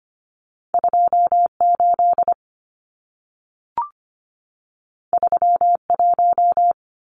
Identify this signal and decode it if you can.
Morse code, heard as a keyed tone, about 25 wpm, 700 Hz. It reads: 28 31
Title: Numbers - 2-Digit 25wpm